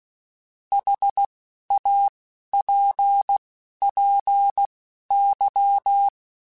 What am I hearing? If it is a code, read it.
Morse code, heard as a keyed tone, about 16 wpm, spaced as for 11 wpm, 800 Hz. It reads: HAPPY
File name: Short Runs / Short Runs-mp3s / Words @10